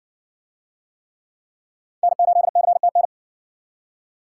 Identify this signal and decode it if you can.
Morse code, heard as a keyed tone, about 60 wpm, 700 Hz. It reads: N8CTN